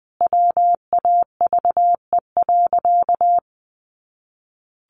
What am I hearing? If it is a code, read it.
Morse code, heard as a keyed tone, about 20 wpm, 700 Hz. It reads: WAVE.
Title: Sentences from Top 1400 Words 20wpm